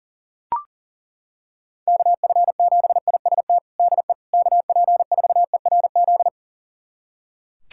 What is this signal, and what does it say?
Morse code, heard as a keyed tone, about 40 wpm, 700 Hz. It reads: KF7IST DE KP4ERZ
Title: USA Callsign DE Callsign - 6 Characters - 40wpm